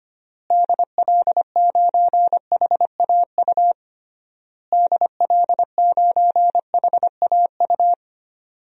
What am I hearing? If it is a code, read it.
Morse code, heard as a keyed tone, about 25 wpm, 700 Hz. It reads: DL9HAU DL9HAU
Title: DXCC-100 Call Signs - 6 Characters - Repeated 3x - 25wpm